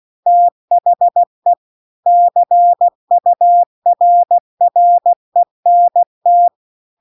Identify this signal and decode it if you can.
Morse code, heard as a keyed tone, about 16 wpm, 700 Hz. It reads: THE CURRENT